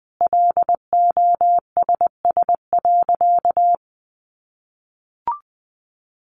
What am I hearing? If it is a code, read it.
Morse code, heard as a keyed tone, about 20 wpm, 700 Hz. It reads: LOSS.
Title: Sentences from Top 1400 Words 20wpm